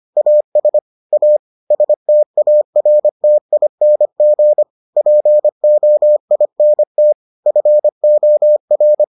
Morse code, heard as a keyed tone, about 25 wpm, 600 Hz. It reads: AS A STARTING POINT FOR